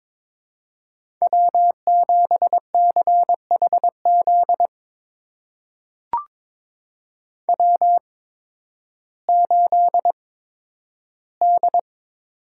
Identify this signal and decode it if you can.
Morse code, heard as a keyed tone, about 21 wpm, spaced as for 12 wpm, 700 Hz. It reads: W7CHZ W8D
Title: USA Call Signs - 5 Characters - 8x Character Spacing - 22wpm